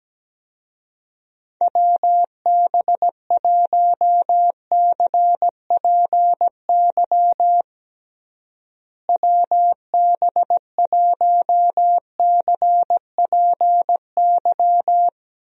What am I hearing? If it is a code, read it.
Morse code, heard as a keyed tone, about 17 wpm, 700 Hz. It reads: WB1CPY WB1CPY
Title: USA Call Signs - 6 Characters - Repeated 2x - 17wpm